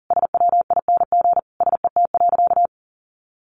Morse code, heard as a keyed tone, about 37 wpm, 700 Hz. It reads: SWING SET.
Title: Additional Easy Phrases 40wpm